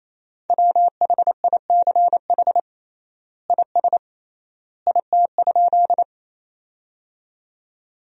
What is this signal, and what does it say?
Morse code, heard as a keyed tone, about 28 wpm, 700 Hz. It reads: WHICH IS IT?
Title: 3x Word-Spacing - Sentences from Top 100 Words - 28wpm